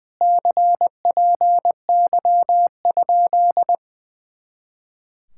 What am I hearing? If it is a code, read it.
Morse code, heard as a keyed tone, about 20 wpm, 700 Hz. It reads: CPY?
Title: Speed-Racing Common QSO Phrases - 20wpm